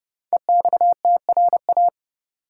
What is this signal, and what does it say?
Morse code, heard as a keyed tone, about 30 wpm, 700 Hz. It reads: EXTRA
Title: Sentences from Top 1000 Words - New Word Drill - 30wpm